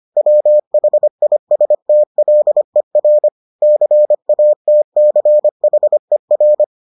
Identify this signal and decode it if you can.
Morse code, heard as a keyed tone, about 25 wpm, 600 Hz. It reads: WHISTLER CATCHER